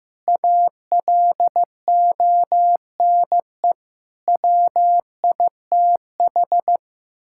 Morse code, heard as a keyed tone, about 15 wpm, 700 Hz. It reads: ALONE WITH
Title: Sentences from Top 1000 Words - Review Part 5 - 15wpm